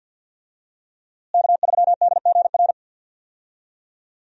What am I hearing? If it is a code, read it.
Morse code, heard as a keyed tone, about 48 wpm, 700 Hz. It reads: K3DGR
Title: USA Call Signs - 5 Characters - Repeated 3x - 50wpm